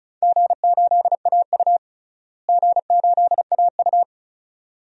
Morse code, heard as a keyed tone, about 36 wpm, 700 Hz. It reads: G8AU G8AU
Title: DXCC-100 Call Signs - 4 Characters - Repeated 2x - 35wpm